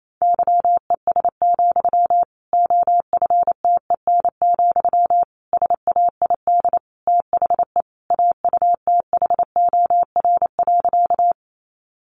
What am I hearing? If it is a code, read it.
Morse code, heard as a keyed tone, about 28 wpm, 700 Hz. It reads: YES, OFTEN, SAID THE AUTHOR.
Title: Sentences from Top 1200 Words 28wpm